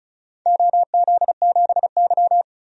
Morse code, heard as a keyed tone, about 35 wpm, 700 Hz. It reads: OZ7Y
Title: DXCC-100 Call Signs - 4 Characters - 35wpm